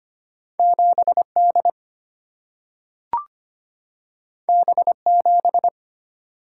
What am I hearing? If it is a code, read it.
Morse code, heard as a keyed tone, about 25 wpm, 700 Hz. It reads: 7 D B 7
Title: Single-Letter Single-Number 25wpm